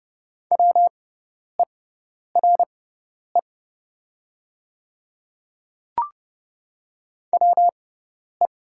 Morse code, heard as a keyed tone, about 30 wpm, spaced as for 10 wpm, 700 Hz. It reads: WERE WE
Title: 6x Character Spacing - Top 200 Words - 30wpm